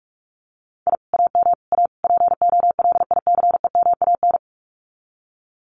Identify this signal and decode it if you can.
Morse code, heard as a keyed tone, about 46 wpm, 700 Hz. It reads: I AM A POLICEMAN
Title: Common QSO Phrases Encore 45wpm